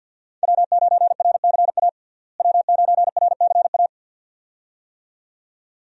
Morse code, heard as a keyed tone, about 51 wpm, 700 Hz. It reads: W9RCA W9RCA
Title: USA Call Signs - 5 Characters - Repeated 2x - 50wpm